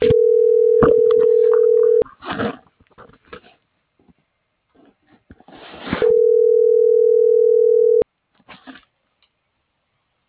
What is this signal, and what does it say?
Ringback tone (North America), 440 + 480 Hz.